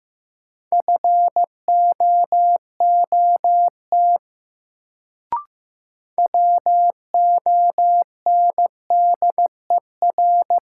Morse code, heard as a keyed tone, about 15 wpm, 700 Hz. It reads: FOOT WONDER
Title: Sentences from Top 500 Words - New Word Drill - 15wpm